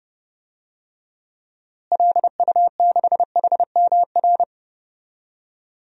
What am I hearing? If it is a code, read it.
Morse code, heard as a keyed tone, about 30 wpm, 700 Hz. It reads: LU6HMR
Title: DXCC-100 Call Signs - 6 Characters - Repeated 3x - 30wpm